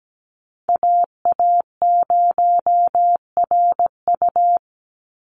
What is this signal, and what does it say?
Morse code, heard as a keyed tone, about 17 wpm, 700 Hz. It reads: AA0RU
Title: USA Call Signs 17wpm